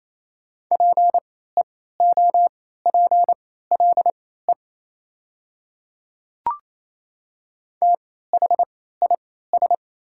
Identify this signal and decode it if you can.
Morse code, heard as a keyed tone, about 28 wpm, spaced as for 16 wpm, 690 Hz. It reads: PEOPLE THIS
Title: 3x Character Spacing - Top 100 Words - 28wpm